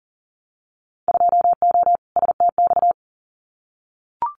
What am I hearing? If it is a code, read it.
Morse code, heard as a keyed tone, about 38 wpm, 700 Hz. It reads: 2O STX
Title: Winter Field Day Exchanges 40wpm